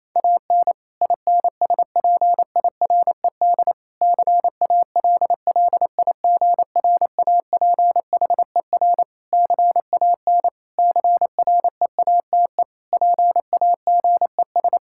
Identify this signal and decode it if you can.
Morse code, heard as a keyed tone, about 28 wpm, 700 Hz. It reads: AN INSPIRED CALLIGRAPHER CAN CREATE PAGES
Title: Pangram Challenge v2 - 28wpm